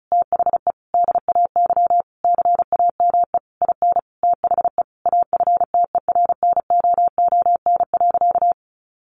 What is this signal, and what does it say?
Morse code, heard as a keyed tone, about 34 wpm, 700 Hz. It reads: THE DAY CAME IN THE AFTERNOON.
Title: Sentences from Top 1500 Words 35wpm